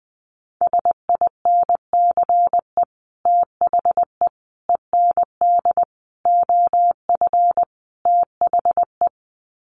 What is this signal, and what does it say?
Morse code, heard as a keyed tone, about 20 wpm, 700 Hz. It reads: SINCE THE END OF THE
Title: Sets of 5 Words Encore 20wpm